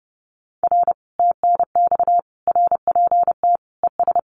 Morse code, heard as a keyed tone, about 32 wpm, 700 Hz. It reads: R TNX RPT ES
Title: Common QSO Phrases Encore 30wpm